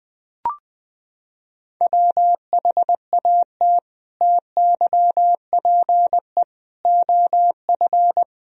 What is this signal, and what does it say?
Morse code, heard as a keyed tone, about 20 wpm, 700 Hz. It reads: WHAT TYPE OF
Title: Sentences from Top 1000 Words - Review Part 3 - 20wpm